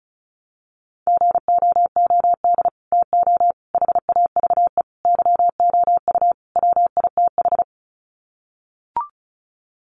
Morse code, heard as a keyed tone, about 34 wpm, 700 Hz. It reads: GOOD TO HAVE YOU WITH
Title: Sets of 5 Words Encore2 35wpm